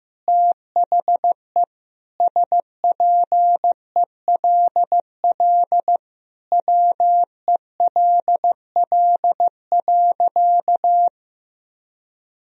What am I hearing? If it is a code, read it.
Morse code, heard as a keyed tone, about 15 wpm, 700 Hz. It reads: THE SPELL WELL.
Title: Sentences from Top 1000 Words - Review Part 1 - 15wpm